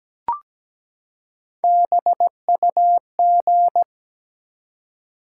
Morse code, heard as a keyed tone, about 17 wpm, 700 Hz. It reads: BUG
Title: Three Letter Words - 17wpm